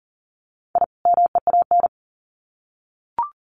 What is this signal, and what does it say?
Morse code, heard as a keyed tone, about 40 wpm, 700 Hz. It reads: I MEAN
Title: Sets of 2 Words 40wpm